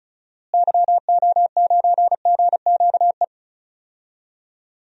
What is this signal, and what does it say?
Morse code, heard as a keyed tone, about 34 wpm, 700 Hz. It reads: YO9GQE
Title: DXCC-100 Call Signs - 6 Characters - Repeated 2x - 35wpm